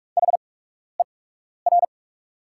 Morse code, heard as a keyed tone, about 44 wpm, spaced as for 12 wpm, 690 Hz. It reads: HER
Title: 8x Character Spacing - Top 100 Words - 45wpm